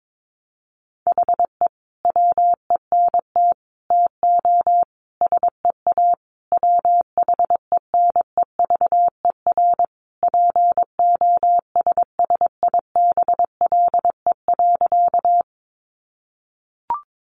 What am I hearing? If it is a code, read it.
Morse code, heard as a keyed tone, about 22 wpm, 700 Hz. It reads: HE WENT TO SEA WHENEVER POSSIBLE.
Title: Sentences from Top 1200 Words 22wpm